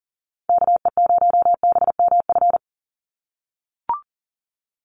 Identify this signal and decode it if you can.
Morse code, heard as a keyed tone, about 40 wpm, 700 Hz. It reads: KE0BMF
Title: USA Call Signs 40wpm